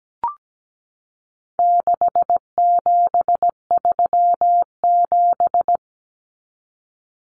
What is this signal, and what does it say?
Morse code, heard as a keyed tone, about 17 wpm, 700 Hz. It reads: 6737
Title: Common Cut Numbers - 4-Digit 17wpm